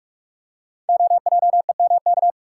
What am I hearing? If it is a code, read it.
Morse code, heard as a keyed tone, about 48 wpm, 700 Hz. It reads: OJEMK
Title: Code Groups - 5 Letters - 45wpm